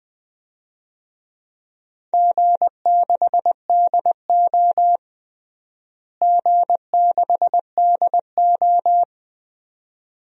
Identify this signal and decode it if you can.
Morse code, heard as a keyed tone, about 20 wpm, 700 Hz. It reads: G6DO G6DO
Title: DXCC-100 Call Signs - 4 Characters - Repeated 2x - 20wpm